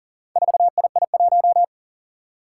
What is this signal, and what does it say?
Morse code, heard as a keyed tone, about 39 wpm, 700 Hz. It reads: 4II1
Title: Four Challenging Letter-Number Combinations 40wpm